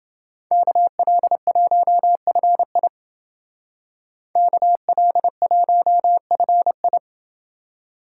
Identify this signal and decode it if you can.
Morse code, heard as a keyed tone, about 29 wpm, 700 Hz. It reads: KL1FI KL1FI